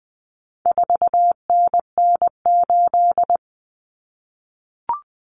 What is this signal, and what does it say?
Morse code, heard as a keyed tone, about 20 wpm, 700 Hz. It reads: 4NN8
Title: Common Cut Numbers - 4-Digit 20wpm